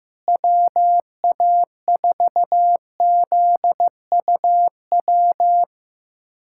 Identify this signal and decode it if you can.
Morse code, heard as a keyed tone, about 15 wpm, 700 Hz. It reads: WA4ZUW
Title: USA Callsign DE Callsign 15wpm